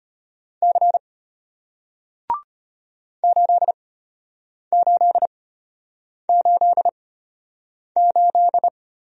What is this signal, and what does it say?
Morse code, heard as a keyed tone, about 34 wpm, spaced as for 8 wpm, 700 Hz. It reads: C 8888